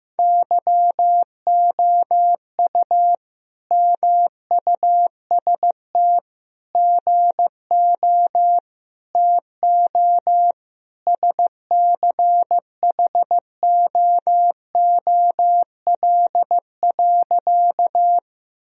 Morse code, heard as a keyed tone, about 16 wpm, 700 Hz. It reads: YOU MUST GO TO SCHOOL.
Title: Sentences from Top 1000 Words - Review Part 1 - 15wpm